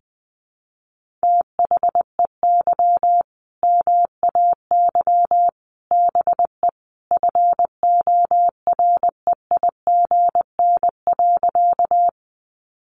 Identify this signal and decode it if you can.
Morse code, heard as a keyed tone, about 20 wpm, 700 Hz. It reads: THEY MAY BE FOREIGN.
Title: Sentences from Top 1400 Words 20wpm